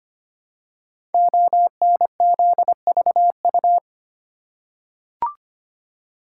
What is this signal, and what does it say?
Morse code, heard as a keyed tone, about 25 wpm, 700 Hz. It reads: ONZVU
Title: Code Groups - 5 Letter:Numbers - 25wpm